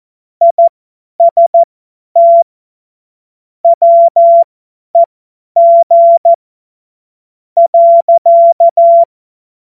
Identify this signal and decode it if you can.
Morse code, heard as a keyed tone, about 14 wpm, spaced as for 10 wpm, 680 Hz. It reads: IST WEG .
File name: Morsekurs / Witze / Witz 23. Mai 2023 50 F70